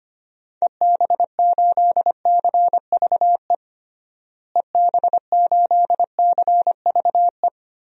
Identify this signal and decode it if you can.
Morse code, heard as a keyed tone, about 25 wpm, 700 Hz. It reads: EB8CVE EB8CVE
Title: DXCC-100 Call Signs - 6 Characters - Repeated 3x - 25wpm